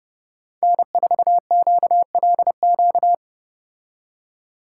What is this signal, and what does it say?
Morse code, heard as a keyed tone, about 30 wpm, 700 Hz. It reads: N4QLQ